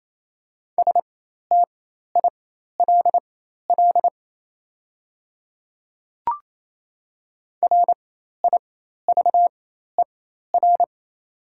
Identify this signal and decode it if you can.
Morse code, heard as a keyed tone, about 27 wpm, spaced as for 13 wpm, 700 Hz. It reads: STILL RIVER
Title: 4x Character Spacing - Top 500 Words - 28wpm